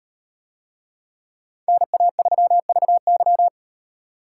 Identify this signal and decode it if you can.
Morse code, heard as a keyed tone, about 38 wpm, 700 Hz. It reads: NA3VY